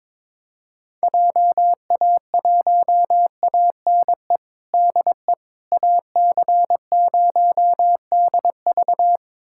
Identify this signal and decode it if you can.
Morse code, heard as a keyed tone, about 21 wpm, 700 Hz. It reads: JA1ANE DE AC0DV
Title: USA Callsign DE Callsign 22wpm